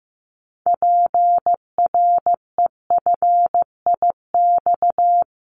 Morse code, heard as a keyed tone, about 15 wpm, 700 Hz. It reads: PREFIX
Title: Callsign Prefixes for Top 25 DXCC - Repeated 3x - 15wpm